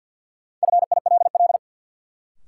Morse code, heard as a keyed tone, about 53 wpm, 700 Hz. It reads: FILL
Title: Sentences from Top 1000 Words - Review Part 3 - New Word Drill - 50wpm